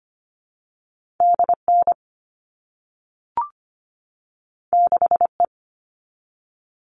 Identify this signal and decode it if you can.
Morse code, heard as a keyed tone, about 25 wpm, 700 Hz. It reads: DN 6E
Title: All Cut Numbers - 2-Digit 25wpm